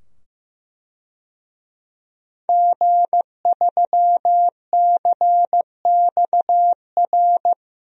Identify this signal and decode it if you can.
Morse code, heard as a keyed tone, about 15 wpm, 700 Hz. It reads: G3CXR
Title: DXCC-100 Call Signs - 5 Characters - 15wpm